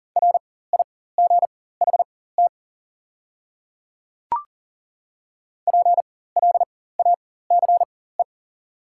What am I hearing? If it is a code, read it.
Morse code, heard as a keyed tone, about 40 wpm, spaced as for 19 wpm, 690 Hz. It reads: RIGHT PLACE
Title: 4x Character Spacing - Top 200 Words - 40wpm